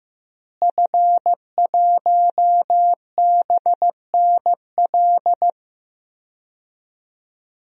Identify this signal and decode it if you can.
Morse code, heard as a keyed tone, about 15 wpm, 700 Hz. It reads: F1BNL
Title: DXCC-100 Call Signs - 5 Characters - Repeated 2x - 15wpm